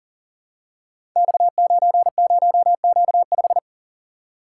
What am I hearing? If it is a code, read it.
Morse code, heard as a keyed tone, about 39 wpm, 700 Hz. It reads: X90Q5
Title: Code Groups - 5 Letter:Numbers - 40wpm